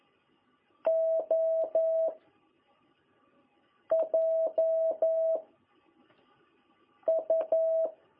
Morse code, heard as a keyed tone, about 11 wpm, 660 Hz. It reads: O J U
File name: jh8cbh / 2021cw